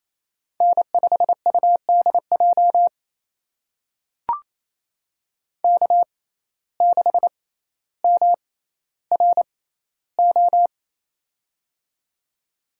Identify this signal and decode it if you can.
Morse code, heard as a keyed tone, about 28 wpm, spaced as for 14 wpm, 700 Hz. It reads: N5UDJ K6MRO
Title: USA Call Signs - 6x Character Spacing - 28wpm